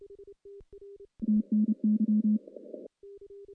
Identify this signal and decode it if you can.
Morse code, heard as a keyed tone, about 30 wpm, 210 Hz. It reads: TNY
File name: contests / CW-SS-k5tr- -1300